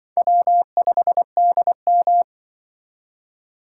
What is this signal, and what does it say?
Morse code, heard as a keyed tone, about 24 wpm, 700 Hz. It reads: W5DM
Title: Speed Racing - USA Call Signs 20wpm